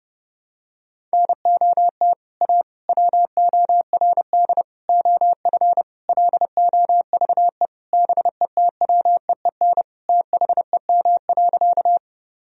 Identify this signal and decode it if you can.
Morse code, heard as a keyed tone, about 30 wpm, 700 Hz. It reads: NOT A WORD OF LOVE BETWEEN THEM.
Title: Sentences from Top 1000 Words - Review Part 1 - 30wpm